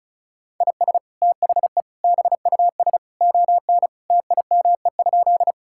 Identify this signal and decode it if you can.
Morse code, heard as a keyed tone, about 36 wpm, 700 Hz. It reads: IS THE BUS ON TIME?
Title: Sentences from Top 1000 Words - Review Part 6 - 35wpm